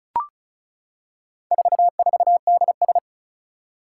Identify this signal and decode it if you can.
Morse code, heard as a keyed tone, about 36 wpm, 700 Hz. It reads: 44DS
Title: Four Challenging Letter-Number Combinations 35wpm